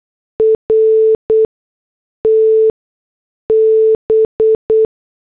Morse code, heard as a keyed tone, about 8 wpm, spaced as for 6 wpm, 440 Hz. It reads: RTB